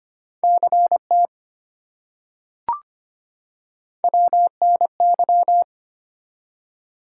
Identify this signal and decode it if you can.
Morse code, heard as a keyed tone, about 25 wpm, 700 Hz. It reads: CT WNY
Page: morse-code-ninja-practice-25wpm